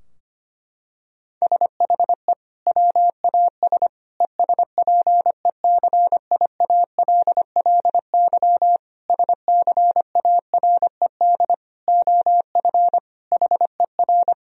Morse code, heard as a keyed tone, about 25 wpm, 700 Hz. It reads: SHE WAS ESPECIALLY SCARED OF HER
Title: Sentences from Top 1000 Words - Review Part 4 - 25wpm